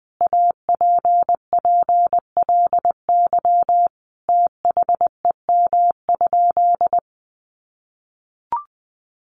Morse code, heard as a keyed tone, about 20 wpm, 700 Hz. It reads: APPLY THEM?
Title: Sentences from Top 1400 Words 20wpm